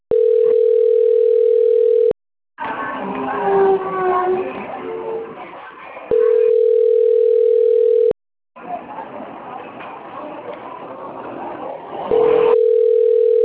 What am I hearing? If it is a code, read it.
Ringback tone (North America), 440 + 480 Hz.
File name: rg-1600-unknown- - - .74176